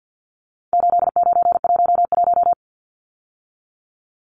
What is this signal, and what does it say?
Morse code, heard as a keyed tone, about 48 wpm, 700 Hz. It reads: 8911
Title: Numbers - 4-Digit 50wpm